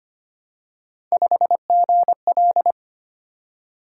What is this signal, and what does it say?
Morse code, heard as a keyed tone, about 25 wpm, 700 Hz. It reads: 5GL
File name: Three Letter-Number Combinations - 25wpm